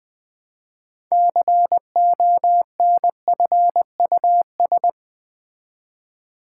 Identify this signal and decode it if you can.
Morse code, heard as a keyed tone, about 20 wpm, 700 Hz. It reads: CONFUS
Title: Build A Word - 2 Letters At A Time - 8-Letter Words - 20wpm